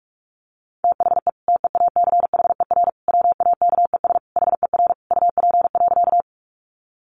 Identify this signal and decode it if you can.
Morse code, heard as a keyed tone, about 46 wpm, 700 Hz. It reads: THE TEACHER WAKES HER UP.
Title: Easy Phrases 45wpm